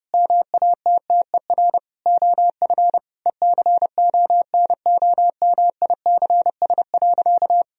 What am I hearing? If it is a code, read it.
Morse code, heard as a keyed tone, about 30 wpm, 700 Hz. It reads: MATTER OF ECONOMICS.
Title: Sentences from Top 1000 Words - Review Part 3 - 30wpm